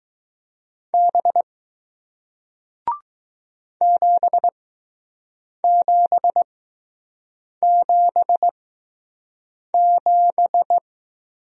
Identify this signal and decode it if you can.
Morse code, heard as a keyed tone, about 20 wpm, spaced as for 6 wpm, 700 Hz. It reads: B 7777